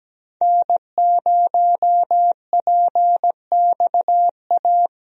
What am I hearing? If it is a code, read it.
Morse code, heard as a keyed tone, about 17 wpm, 700 Hz. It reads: N0PXA
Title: USA Call Signs - 5 Characters - Repeated 3x - 17wpm